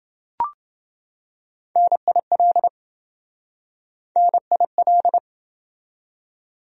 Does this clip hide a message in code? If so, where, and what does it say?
Yes: Morse code from 1.8 to 5.2 s, heard as a keyed tone, about 28 wpm, 700 Hz. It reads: NIL NIL